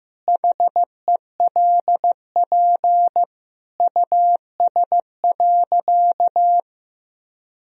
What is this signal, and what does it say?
Morse code, heard as a keyed tone, about 15 wpm, 700 Hz. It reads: HELP US.
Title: Sentences from Top 1000 Words - Review Part 2 - 15wpm